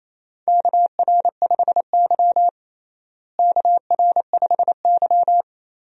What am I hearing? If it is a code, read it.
Morse code, heard as a keyed tone, about 28 wpm, 700 Hz. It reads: KR5Y KR5Y
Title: USA Call Signs - 4 Characters - Repeated 2x - 28wpm